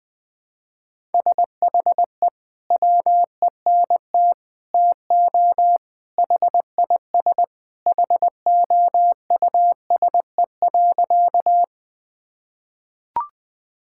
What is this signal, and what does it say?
Morse code, heard as a keyed tone, about 20 wpm, 700 Hz. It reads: SHE WENT TO HIS HOUSE.
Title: Sentences with 2-Word Phrases - 20wpm